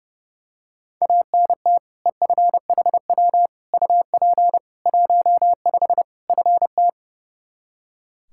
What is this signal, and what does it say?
Morse code, heard as a keyed tone, about 30 wpm, 700 Hz. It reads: ANT EFHW UP 15 FT
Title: Practice QSO - Exchange 2 - Info - 30wpm